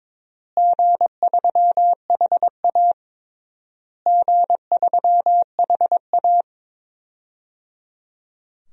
Morse code, heard as a keyed tone, about 22 wpm, 700 Hz. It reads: G3HA G3HA
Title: DXCC-100 Call Signs - 4 Characters - Repeated 3x - 22wpm